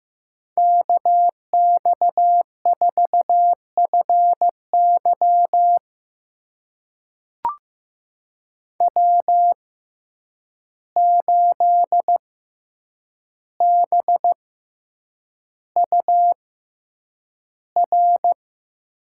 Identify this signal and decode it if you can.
Morse code, heard as a keyed tone, about 15 wpm, spaced as for 8 wpm, 700 Hz. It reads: KX4FY W8BUR
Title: USA Call Signs - 5 Characters - 6x Character Spacing - 15wpm